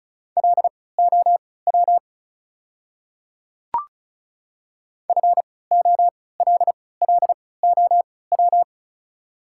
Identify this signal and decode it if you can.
Morse code, heard as a keyed tone, about 34 wpm, spaced as for 20 wpm, 700 Hz. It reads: LOW FOLLOW
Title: 3x Character Spacing - Top 500 Words - 35wpm